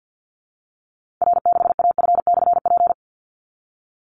Morse code, heard as a keyed tone, about 49 wpm, 700 Hz. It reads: R6AF/P
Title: Call Signs Repeated 3x - 50wpm